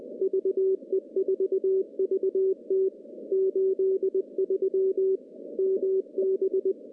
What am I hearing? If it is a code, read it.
Morse code, heard as a keyed tone, about 18 wpm, 380 Hz. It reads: VE4VT 83 MB
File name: contests / CW-SS-k5tr- -0200